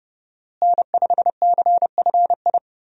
Morse code, heard as a keyed tone, about 32 wpm, 700 Hz. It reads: N5CFI
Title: USA Call Signs - 5 Characters - Repeated 3x - 30wpm